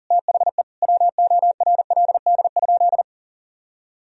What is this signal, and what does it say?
Morse code, heard as a keyed tone, about 40 wpm, 700 Hz. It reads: THE WORLD?
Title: Sentences from Top 200 Words and Abbreviations 40wpm